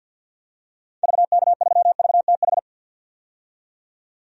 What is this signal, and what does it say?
Morse code, heard as a keyed tone, about 49 wpm, 700 Hz. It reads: VK3VTH
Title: DXCC-100 Call Signs - 6 Characters - 50wpm